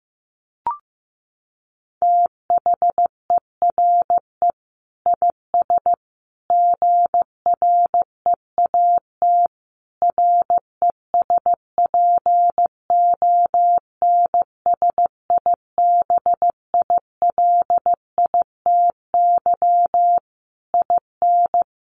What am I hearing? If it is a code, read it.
Morse code, heard as a keyed tone, about 15 wpm, 700 Hz. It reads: THERE IS GREAT RESPONSIBILITY IN
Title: Sentences from Top 1200 Words 15wpm